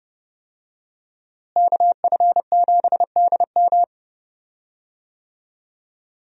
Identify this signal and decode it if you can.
Morse code, heard as a keyed tone, about 30 wpm, 700 Hz. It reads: KF7DM
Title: USA Call Signs - 5 Characters - Repeated 3x - 30wpm